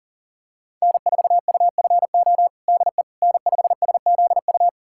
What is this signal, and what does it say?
Morse code, heard as a keyed tone, about 40 wpm, 700 Hz. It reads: N4UFO DE N5SZU
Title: USA Callsign DE Callsign - 5 Characters - 40wpm